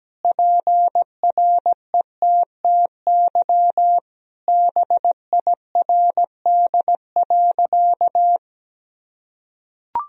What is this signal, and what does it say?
Morse code, heard as a keyed tone, about 17 wpm, 700 Hz. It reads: PRETTY BIRD.
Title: Sentences from Top 1000 Words - Review Part 2 - 17wpm